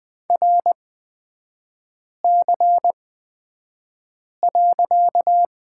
Morse code, heard as a keyed tone, about 20 wpm, spaced as for 5 wpm, 700 Hz. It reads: RC.
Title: 255 Lesson 41e - Period - Three Characters 20wpm